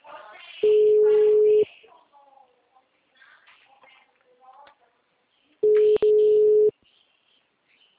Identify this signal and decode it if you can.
Ringback tone (Europe, 425 Hz).